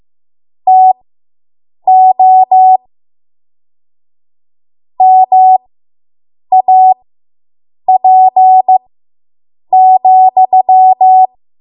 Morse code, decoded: TO MAP,